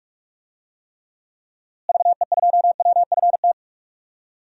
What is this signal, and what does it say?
Morse code, heard as a keyed tone, about 46 wpm, 700 Hz. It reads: VE2WFT
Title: DXCC-100 Call Signs - 6 Characters - Repeated 3x - 45wpm